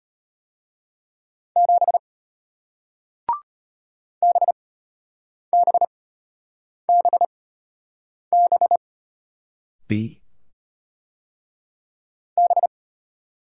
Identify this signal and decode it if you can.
Morse code, heard as a keyed tone, about 33 wpm, spaced as for 8 wpm, 690 Hz. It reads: 7 BBBB B